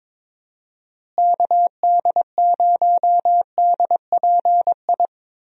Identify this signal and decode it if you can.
Morse code, heard as a keyed tone, about 21 wpm, 700 Hz. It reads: KD0DPI
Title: USA Call Signs - 6 Characters - Repeated 2x - 22wpm